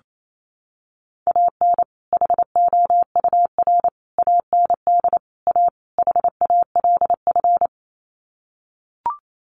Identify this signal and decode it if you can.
Morse code, heard as a keyed tone, about 28 wpm, 700 Hz. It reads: AN HOUR AND A HALF
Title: Sets of 5 Words 28wpm